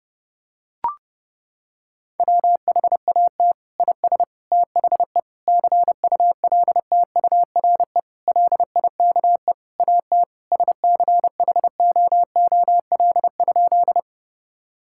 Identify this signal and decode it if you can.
Morse code, heard as a keyed tone, about 28 wpm, 700 Hz. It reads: WHAT IS THE CULTURE LIKE AT SCHOOL?
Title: Sentences from Top 1000 Words - Review Part 6 - 30wpm